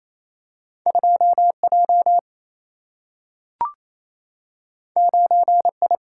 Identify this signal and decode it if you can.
Morse code, heard as a keyed tone, about 28 wpm, 700 Hz. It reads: 2J 9I
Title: Single-Letter Single-Number 28wpm